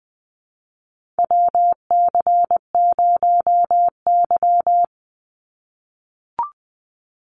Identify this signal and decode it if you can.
Morse code, heard as a keyed tone, about 20 wpm, 700 Hz. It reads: WC0Y
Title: NA SOTA Chaser Callsigns 20wpm